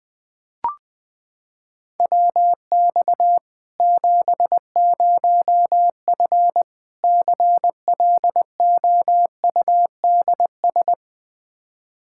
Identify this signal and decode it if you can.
Morse code, heard as a keyed tone, about 21 wpm, 700 Hz. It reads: WX 70F CLOUDS
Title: K1USN SST QSO Practice 20wpm